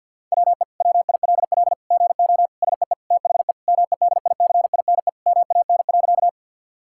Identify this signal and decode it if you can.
Morse code, heard as a keyed tone, about 49 wpm, 700 Hz. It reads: WE WILL GO SEE THE MEDICINE MAN.
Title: Sentences from Top 1000 Words - Review Part 6 - 50wpm